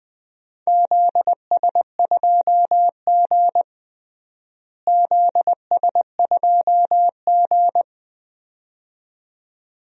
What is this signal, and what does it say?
Morse code, heard as a keyed tone, about 20 wpm, 700 Hz. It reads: ZS2G ZS2G
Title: DXCC-100 Call Signs - 4 Characters - Repeated 2x - 20wpm